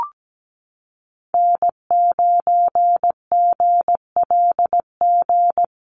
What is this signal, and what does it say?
Morse code, heard as a keyed tone, about 17 wpm, 700 Hz. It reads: N9GLG
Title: USA Call Signs Repeated 2x 17wpm